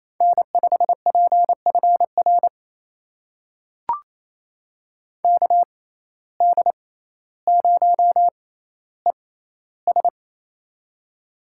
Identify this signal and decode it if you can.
Morse code, heard as a keyed tone, about 28 wpm, spaced as for 14 wpm, 700 Hz. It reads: N5PFR KD0ES
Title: USA Call Signs - 5 Characters - 6x Character Spacing - 28wpm